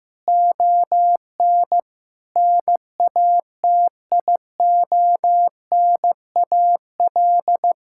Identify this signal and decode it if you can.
Morse code, heard as a keyed tone, about 15 wpm, 700 Hz. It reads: ON NATIONAL
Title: Sentences from Top 1000 Words - Review Part 5 - 15wpm